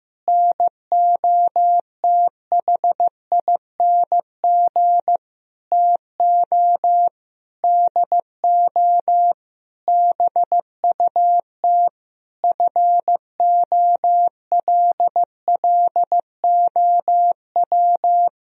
Morse code, heard as a keyed tone, about 16 wpm, 700 Hz. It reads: NOTHING TO DO BUT FOLLOW
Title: Sentences from Top 1000 Words - Review Part 1 - 15wpm